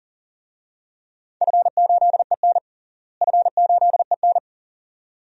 Morse code, heard as a keyed tone, about 40 wpm, 700 Hz. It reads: F8EN F8EN
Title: DXCC-100 Call Signs - 4 Characters - Repeated 2x - 40wpm